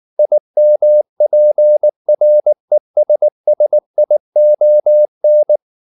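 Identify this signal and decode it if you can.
Morse code, heard as a keyed tone, about 19 wpm, 600 Hz. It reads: IMPRESSION
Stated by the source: Morse code at 19 wpm from CW med Gnister, reading Ken Follett's novel Never in English.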